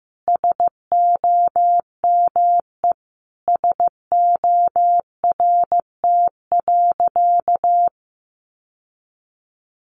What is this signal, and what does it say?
Morse code, heard as a keyed tone, about 15 wpm, 700 Hz. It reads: SOME SORT.
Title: Sentences from Top 1300 Words 15wpm